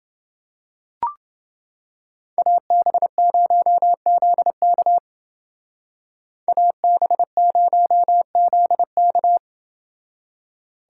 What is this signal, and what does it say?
Morse code, heard as a keyed tone, about 29 wpm, 700 Hz. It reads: AB0ZK AB0ZK